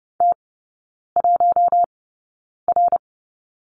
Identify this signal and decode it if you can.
Morse code, heard as a keyed tone, about 30 wpm, spaced as for 9 wpm, 700 Hz. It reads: T1R
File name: 081 Lesson 14e - Number 2 - Three Characters 30wpm